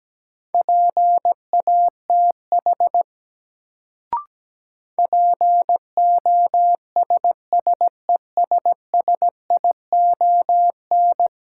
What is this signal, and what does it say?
Morse code, heard as a keyed tone, about 17 wpm, 700 Hz. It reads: PATH POSSESSION
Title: Sentences from Top 1400 Words - New Word Drill - 17wpm